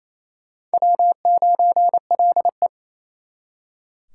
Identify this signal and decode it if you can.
Morse code, heard as a keyed tone, about 38 wpm, spaced as for 33 wpm, 700 Hz. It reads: W9LE
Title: Speed Racing - USA Call Signs 28wpm